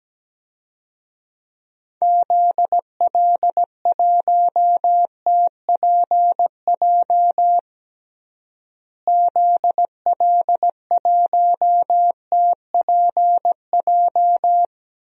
Morse code, heard as a keyed tone, about 17 wpm, 700 Hz. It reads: ZL1TPJ ZL1TPJ
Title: DXCC-100 Call Signs - 6 Characters - Repeated 3x - 17wpm